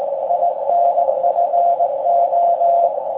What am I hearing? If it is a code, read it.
Morse code, heard as a keyed tone, about 19 wpm, 690 Hz, with noise. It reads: INFO
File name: forum / im / aq grimeton